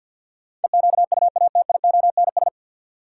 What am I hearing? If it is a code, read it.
Morse code, heard as a keyed tone, about 51 wpm, 700 Hz. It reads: EQUATIONS
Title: Sets of 5 Words - New Word Drill - 50wpm